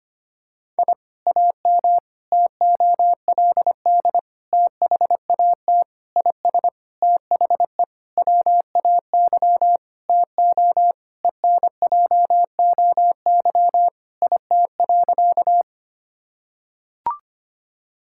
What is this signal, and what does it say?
Morse code, heard as a keyed tone, about 25 wpm, 700 Hz. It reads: I AM TOLD THAT IS THE WAY TO ENJOY IT.
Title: Sentences from Top 1000 Words - Review Part 5 - 25wpm